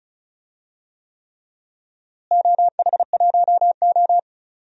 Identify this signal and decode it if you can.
Morse code, heard as a keyed tone, about 36 wpm, 690 Hz. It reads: OH1O
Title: DXCC-100 Call Signs - 4 Characters - Repeated 3x - 35wpm